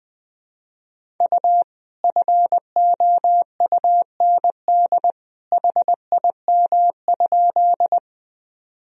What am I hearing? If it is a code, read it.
Morse code, heard as a keyed tone, about 20 wpm, 700 Hz. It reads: U FOUND HIM?
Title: Sentences from Top 300 Words and Abbreviations 20wpm